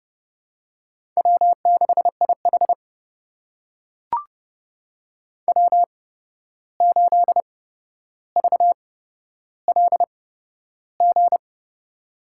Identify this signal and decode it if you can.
Morse code, heard as a keyed tone, about 29 wpm, spaced as for 12 wpm, 700 Hz. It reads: W6IH W8VLG
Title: USA Call Signs - 8x Character Spacing - 30wpm